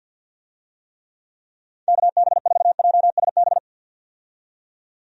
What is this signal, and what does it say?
Morse code, heard as a keyed tone, about 47 wpm, 700 Hz. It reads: KB4JSB
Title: USA Call Signs - 6 Characters - 50wpm